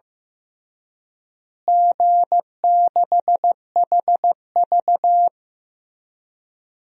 Morse code, heard as a keyed tone, about 15 wpm, 700 Hz. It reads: G6HV
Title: DXCC-100 Call Signs - 4 Characters - Repeated 3x - 15wpm